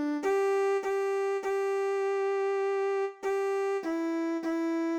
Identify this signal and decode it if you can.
There is no code.